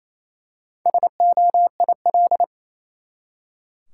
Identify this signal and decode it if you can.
Morse code, heard as a keyed tone, about 28 wpm, 700 Hz. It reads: SOIL
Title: Four Letter Words - 28wpm